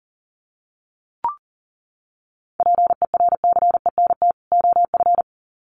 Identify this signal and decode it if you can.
Morse code, heard as a keyed tone, about 40 wpm, 700 Hz. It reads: PERCENT OF
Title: Sets of 2 Words 40wpm